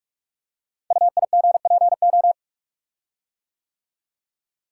Morse code, heard as a keyed tone, about 44 wpm, 700 Hz. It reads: UIGPO
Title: Code Groups - 5 Letters - 45wpm